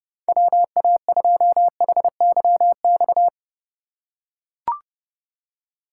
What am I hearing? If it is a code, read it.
Morse code, heard as a keyed tone, about 30 wpm, 700 Hz. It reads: WA2HYX
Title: USA Call Signs - 6 Characters - 3x Character Spacing - 30wpm